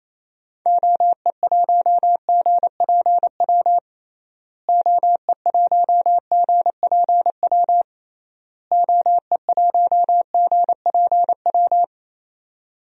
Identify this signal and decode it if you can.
Morse code, heard as a keyed tone, about 28 wpm, 700 Hz. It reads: OE1GPW OE1GPW OE1GPW
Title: DXCC-100 Call Signs - 6 Characters - Repeated 3x - 28wpm